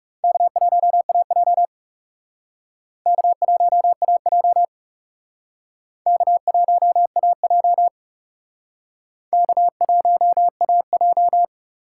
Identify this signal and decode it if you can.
Morse code, heard as a keyed tone, about 39 wpm, 700 Hz. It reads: K1AJ K1AJ K1AJ K1AJ